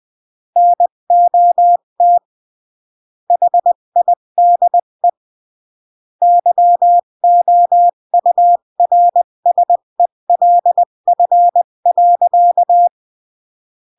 Morse code, heard as a keyed tone, about 20 wpm, 700 Hz. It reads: NOT HIDE YOURSELF.
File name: CW / jodorowsky / 82 commandments 20-15